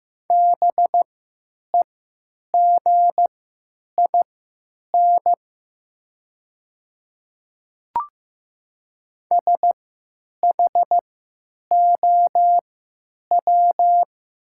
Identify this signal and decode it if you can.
Morse code, heard as a keyed tone, about 15 wpm, spaced as for 9 wpm, 700 Hz. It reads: BEGIN SHOW
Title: 3x Character Spacing - Top 300 Words - 15wpm